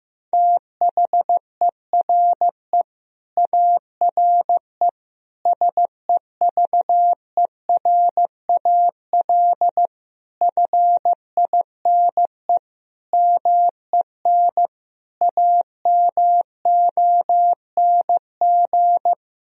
Morse code, heard as a keyed tone, about 15 wpm, 700 Hz. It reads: THERE ARE SEVERAL FINE MEN AMONG
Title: Sentences from Top 1000 Words - Review Part 2 - 15wpm